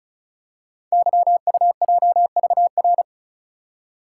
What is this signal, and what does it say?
Morse code, heard as a keyed tone, about 36 wpm, 700 Hz. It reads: YUJVR
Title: Code Groups - 5 Letters - 35wpm